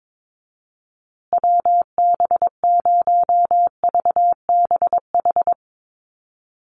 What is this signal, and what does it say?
Morse code, heard as a keyed tone, about 22 wpm, 700 Hz. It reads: WB0VBH
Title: USA Call Signs Repeated 3x 22wpm